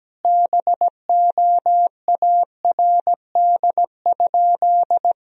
Morse code, heard as a keyed tone, about 17 wpm, 700 Hz. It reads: BOARD?
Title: Sentences from Top 1000 Words - Review Part 4 - 17wpm